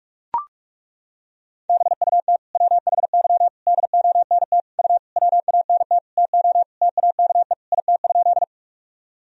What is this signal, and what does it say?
Morse code, heard as a keyed tone, about 42 wpm, 700 Hz. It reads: BUT WHY DONT U WANT TO TAKE IT?
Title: Sentences from Top 300 Words and Abbreviations 45wpm